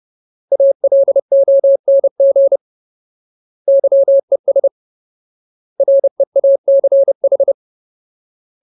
Morse code, heard as a keyed tone, about 30 wpm, 550 Hz. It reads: ALONG YES REACH
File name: Ultimate Word List / 30WPM Ultimate